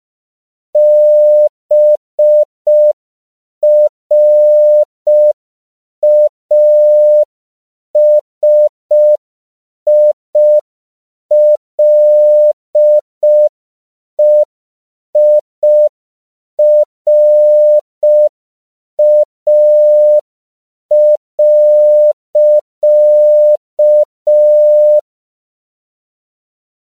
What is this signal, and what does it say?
Morse code, heard as a keyed tone, about 5 wpm, 600 Hz. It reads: BRASILEIRA.
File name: CW / 5PPM